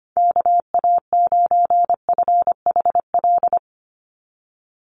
Morse code, heard as a keyed tone, about 25 wpm, 700 Hz. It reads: KA9FHL
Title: USA Call Signs 25wpm